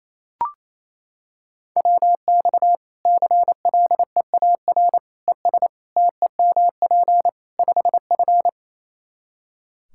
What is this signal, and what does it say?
Morse code, heard as a keyed tone, about 28 wpm, 700 Hz. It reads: WX CLEAR ES TEMP 5F
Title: Practice QSO - Exchange 2 - Info - 28wpm